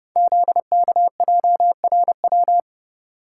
Morse code, heard as a keyed tone, about 30 wpm, 700 Hz. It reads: ZKJRW
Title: Code Groups - 5 Letter:Numbers - 30wpm